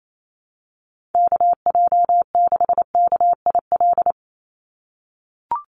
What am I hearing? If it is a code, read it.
Morse code, heard as a keyed tone, about 28 wpm, 700 Hz. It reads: KJ6KIL
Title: USA Call Signs 28wpm